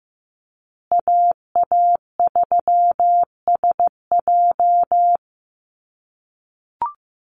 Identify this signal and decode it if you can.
Morse code, heard as a keyed tone, about 15 wpm, 700 Hz. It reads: AA3SJ
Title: USA Call Signs Repeated 3x 15wpm